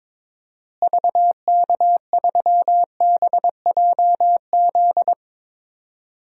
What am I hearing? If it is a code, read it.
Morse code, heard as a keyed tone, about 22 wpm, 700 Hz. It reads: VK3BJZ
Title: DXCC-100 Call Signs - 6 Characters - Repeated 3x - 22wpm